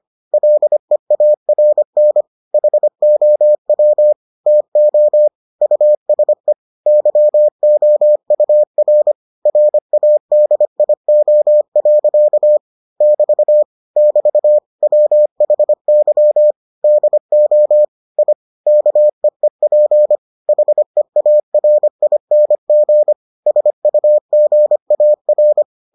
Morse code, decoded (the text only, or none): LEARN HOW TO USE YOUR RADIO. <BT> <BT> WHY DO I KEEP HEARING SUGAR